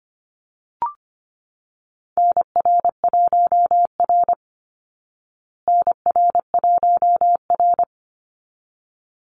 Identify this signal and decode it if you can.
Morse code, heard as a keyed tone, about 24 wpm, 700 Hz. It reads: NR1R NR1R
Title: Call Signs Repeated 3x - 25wpm